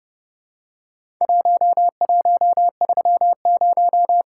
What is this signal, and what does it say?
Morse code, heard as a keyed tone, about 30 wpm, 700 Hz. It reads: 1130
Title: Worked All Europe DX Contest - QTC Exchange - 30wpm